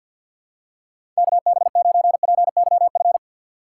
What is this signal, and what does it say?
Morse code, heard as a keyed tone, about 46 wpm, 700 Hz. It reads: KB9PYF
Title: USA Call Signs - 6 Characters - 3x Character Spacing - 50wpm